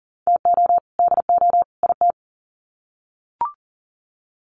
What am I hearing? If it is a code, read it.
Morse code, heard as a keyed tone, about 40 wpm, 700 Hz. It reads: TO DO IT
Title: Sets of 3 Words 40wpm